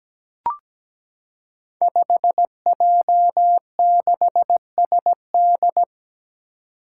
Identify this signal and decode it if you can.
Morse code, heard as a keyed tone, about 17 wpm, 700 Hz. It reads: 5J6SD
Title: Five Challenging Letter-Number Combinations 17wpm